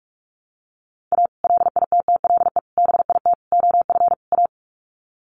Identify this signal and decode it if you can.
Morse code, heard as a keyed tone, about 44 wpm, 700 Hz. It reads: A LITTLE BIT OF A
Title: Sets of 5 Words 45wpm